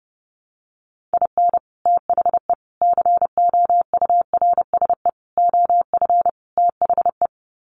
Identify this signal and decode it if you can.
Morse code, heard as a keyed tone, about 30 wpm, 700 Hz. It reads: IN THE COURSE OF THE
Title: Sets of 5 Words 30wpm